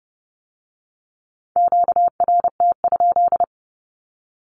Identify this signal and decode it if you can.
Morse code, heard as a keyed tone, about 30 wpm, 690 Hz. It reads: QRT?
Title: Q-Codes with Spoken Meaning - 30wpm